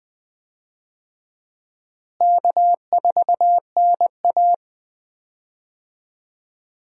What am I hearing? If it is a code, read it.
Morse code, heard as a keyed tone, about 20 wpm, 700 Hz. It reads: K4NA
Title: USA Call Signs - 4 Characters - Repeated 2x - 20wpm